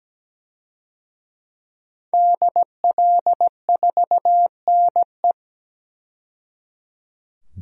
Morse code, heard as a keyed tone, about 17 wpm, 700 Hz. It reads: DL4NE
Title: DXCC-100 Call Signs - 5 Characters - 17wpm